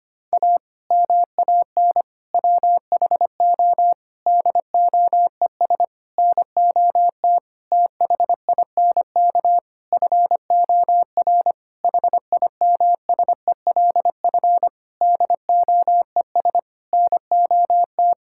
Morse code, heard as a keyed tone, about 25 wpm, 700 Hz. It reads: A MAN WHO DOES NOT THINK FOR HIMSELF DOES NOT
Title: Sentences from Top 1600 Words 25wpm